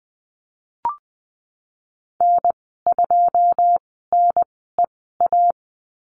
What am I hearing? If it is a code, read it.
Morse code, heard as a keyed tone, about 20 wpm, spaced as for 15 wpm, 700 Hz. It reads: N2NEA
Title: 154 Lesson 25g - Letter p - Call Signs 20wpm